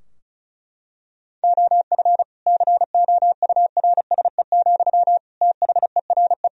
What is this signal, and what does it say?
Morse code, heard as a keyed tone, about 36 wpm, 700 Hz. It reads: OF COURSE, THERE
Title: Sentences from Top 1000 Words - Review Part 4 - 35wpm